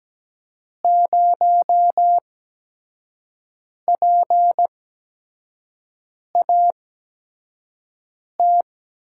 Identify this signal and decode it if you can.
Morse code, heard as a keyed tone, about 17 wpm, spaced as for 5 wpm, 700 Hz. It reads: 0PAT
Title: USA Call Signs - 8x Character Spacing - 17wpm